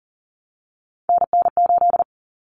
Morse code, heard as a keyed tone, about 40 wpm, 700 Hz. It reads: NN8
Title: Common Cut Numbers - 3-Digit 40wpm